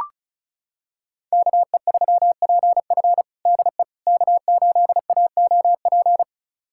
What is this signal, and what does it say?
Morse code, heard as a keyed tone, about 34 wpm, 700 Hz. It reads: KE3PF DE K8AOP
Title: USA Callsign DE Callsign - 5 Characters - 35wpm